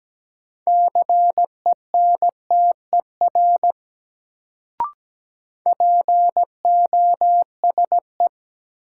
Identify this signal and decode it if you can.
Morse code, heard as a keyed tone, about 17 wpm, 700 Hz. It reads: CENTER POSE
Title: Sentences from Top 1000 Words - Review Part 2 - New Word Drill - 17wpm